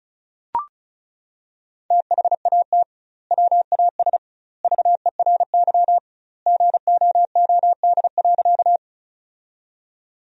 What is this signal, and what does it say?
Morse code, heard as a keyed tone, about 36 wpm, 700 Hz. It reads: THAT WAS VERY GOOD.
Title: 2x Word-Spacing - Sentences from Top 200 Words - 35wpm